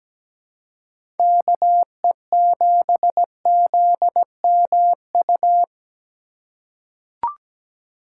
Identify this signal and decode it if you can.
Morse code, heard as a keyed tone, about 17 wpm, 700 Hz. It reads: KE7ZMU
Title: USA Call Signs - 3x Character Spacing - 17wpm